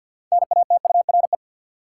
Morse code, heard as a keyed tone, about 52 wpm, 700 Hz. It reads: NATURE
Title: Sets of 3 Words - New Word Drill - 50wpm